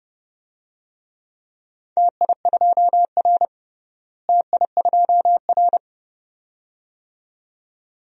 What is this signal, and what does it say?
Morse code, heard as a keyed tone, about 30 wpm, 700 Hz. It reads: TI2R TI2R
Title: DXCC-100 Call Signs - 4 Characters - Repeated 2x - 30wpm